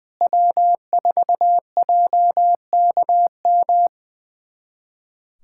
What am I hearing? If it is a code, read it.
Morse code, heard as a keyed tone, about 20 wpm, 700 Hz. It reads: W4JKM
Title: USA Callsign DE Callsign - 5 Characters - 20wpm